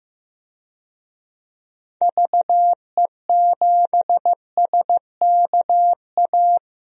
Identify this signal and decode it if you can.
Morse code, heard as a keyed tone, about 15 wpm, 700 Hz. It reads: VE7SKA
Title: DXCC-100 Call Signs - 6 Characters - 15wpm